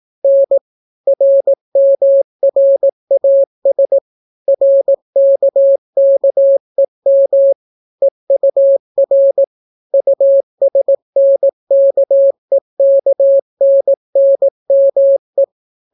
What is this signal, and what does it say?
Morse code, heard as a keyed tone, about 18 wpm, 550 Hz. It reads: N RMRAS RKKEM EUR USNKEKNNME